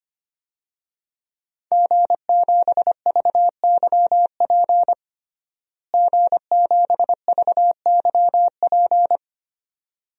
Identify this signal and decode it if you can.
Morse code, heard as a keyed tone, about 25 wpm, 700 Hz. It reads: G7VYP G7VYP
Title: DXCC-100 Call Signs - 5 Characters - Repeated 2x - 25wpm